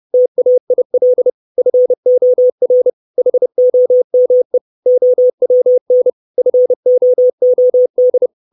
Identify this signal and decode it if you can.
Morse code, heard as a keyed tone, about 30 wpm, 500 Hz. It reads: TAIL FOR HOME OWN FOOD
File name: 632 Words 30WPM